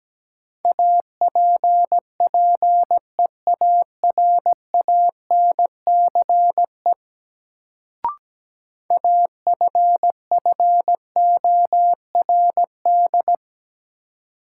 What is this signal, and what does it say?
Morse code, heard as a keyed tone, about 17 wpm, 700 Hz. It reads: APPEARANCE AFFORD
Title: Sentences from Top 1100 Words - New Word Drill Alt - 17wpm